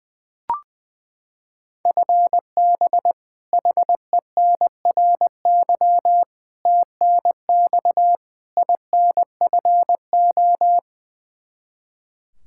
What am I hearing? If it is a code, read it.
Morse code, heard as a keyed tone, about 21 wpm, 700 Hz. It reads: FB HENRY TNX INFO
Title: Practice QSO - Exchange 3 - More Info - 20wpm